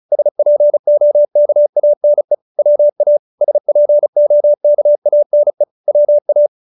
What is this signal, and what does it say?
Morse code, heard as a keyed tone, about 34 wpm, 590 Hz. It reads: SPOKANE WA SPOKANE WA